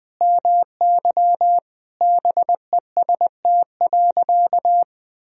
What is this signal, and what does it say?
Morse code, heard as a keyed tone, about 20 wpm, 700 Hz. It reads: MY BEST.
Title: Sentences from Top 1000 Words - Review Part 2 - 20wpm